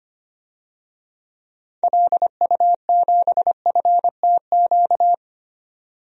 Morse code, heard as a keyed tone, about 25 wpm, 700 Hz. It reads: LU7FTQ
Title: DXCC-100 Call Signs - 6 Characters - 25wpm